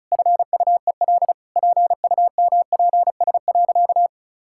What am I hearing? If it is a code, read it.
Morse code, heard as a keyed tone, about 36 wpm, 700 Hz. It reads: FUEL PUMPS.
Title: Pangram Challenge v2 - 35wpm